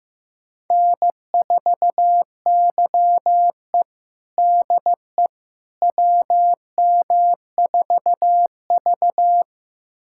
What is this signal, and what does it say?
Morse code, heard as a keyed tone, about 15 wpm, 700 Hz. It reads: N4YE DE WM4V
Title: USA Callsign DE Callsign - 4 Characters - 15wpm